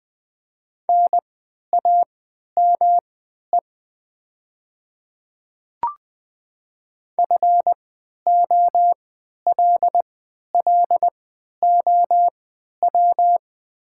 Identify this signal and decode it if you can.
Morse code, heard as a keyed tone, about 20 wpm, spaced as for 11 wpm, 700 Hz. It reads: NAME FOLLOW
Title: 3x Character Spacing - Top 200 Words - 20wpm